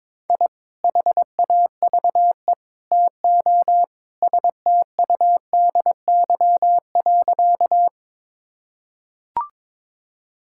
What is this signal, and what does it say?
Morse code, heard as a keyed tone, about 22 wpm, 700 Hz. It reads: I HAVE TO STUDY.
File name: practice / 22wpm / Sentences with 3-Word Phrases - 22wpm